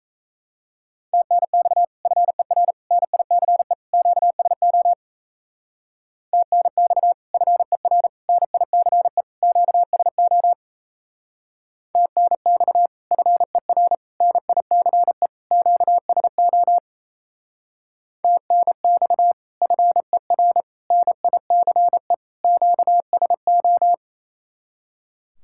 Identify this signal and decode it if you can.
Morse code, heard as a keyed tone, about 36 wpm, 700 Hz. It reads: TNX FER NICE QSO TNX FER NICE QSO TNX FER NICE QSO TNX FER NICE QSO